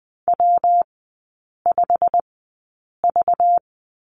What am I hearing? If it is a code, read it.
Morse code, heard as a keyed tone, about 20 wpm, spaced as for 8 wpm, 700 Hz. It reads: W5V
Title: 191 Lesson 31e - Letter v - Three Characters 20wpm